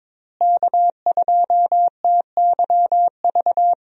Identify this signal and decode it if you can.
Morse code, heard as a keyed tone, about 22 wpm, 700 Hz. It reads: K2TYV
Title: USA Call Signs - 5 Characters - Repeated 2x - 22wpm